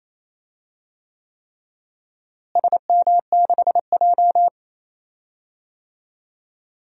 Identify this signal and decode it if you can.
Morse code, heard as a keyed tone, about 28 wpm, 700 Hz. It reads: SM6J
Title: DXCC-100 Call Signs - 4 Characters - Repeated 3x - 28wpm